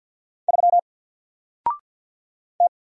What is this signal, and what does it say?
Morse code, heard as a keyed tone, about 48 wpm, 700 Hz. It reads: 3 T
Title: Single Character (Letter:Number:Punctuation:Prosign) 50wpm